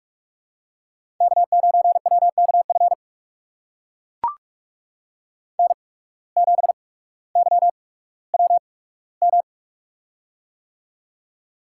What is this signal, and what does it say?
Morse code, heard as a keyed tone, about 44 wpm, spaced as for 19 wpm, 690 Hz. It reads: K9WKF N7YWM
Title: USA Call Signs - 5 Characters - 8x Character Spacing - 45wpm